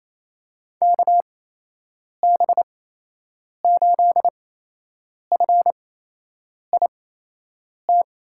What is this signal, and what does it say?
Morse code, heard as a keyed tone, about 28 wpm, spaced as for 8 wpm, 700 Hz. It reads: KB8FIT
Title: USA Call Signs - 6 Characters - 8x Character Spacing - 28wpm